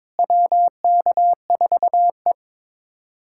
Morse code, heard as a keyed tone, about 22 wpm, 700 Hz. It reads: WK4E
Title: USA Callsign DE Callsign - 4 Characters - 22wpm